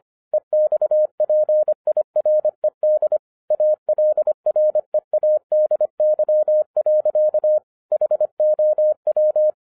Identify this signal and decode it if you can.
Morse code, heard as a keyed tone, about 25 wpm, 610 Hz. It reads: EXPIRED ALREADY. HOW